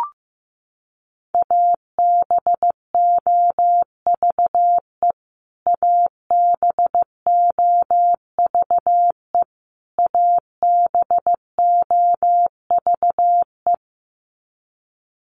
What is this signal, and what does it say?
Morse code, heard as a keyed tone, about 15 wpm, 700 Hz. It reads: ABOVE ABOVE ABOVE
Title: Top 500 Words Repeated 15wpm